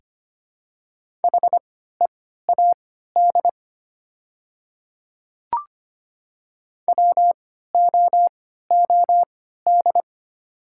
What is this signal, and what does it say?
Morse code, heard as a keyed tone, about 25 wpm, spaced as for 14 wpm, 700 Hz. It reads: HEAD WOOD
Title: 3x Character Spacing - Top 300 Words - 25wpm